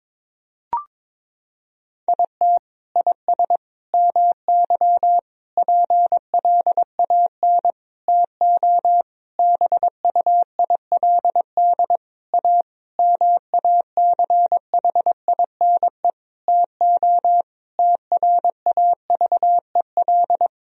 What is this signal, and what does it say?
Morse code, heard as a keyed tone, about 21 wpm, 700 Hz. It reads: IT IS MY PLAN TO BUILD A MACHINE TO TRAVEL
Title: Sentences from Top 1000 Words - Review Part 2 - 22wpm